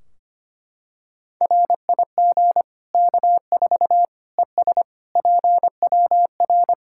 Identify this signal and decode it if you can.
Morse code, heard as a keyed tone, about 25 wpm, 700 Hz. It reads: RIG K4 ES PWR
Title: Practice QSO - Exchange 2 - Info - 25wpm